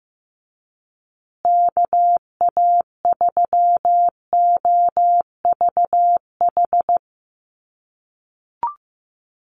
Morse code, heard as a keyed tone, about 15 wpm, 700 Hz. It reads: KA3OVH
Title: USA Call Signs Repeated 2x 15wpm